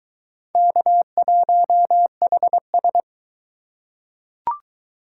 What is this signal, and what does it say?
Morse code, heard as a keyed tone, about 22 wpm, 700 Hz. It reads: K1HS